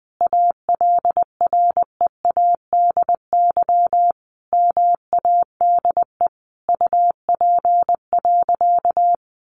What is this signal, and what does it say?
Morse code, heard as a keyed tone, about 20 wpm, 700 Hz. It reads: ALREADY MADE UP.
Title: Sentences from Top 700 Words 20wpm